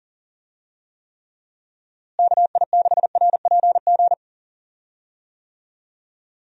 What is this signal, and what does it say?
Morse code, heard as a keyed tone, about 38 wpm, 700 Hz. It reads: KI6RPG
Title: USA Call Signs - 6 Characters - Repeated 2x - 40wpm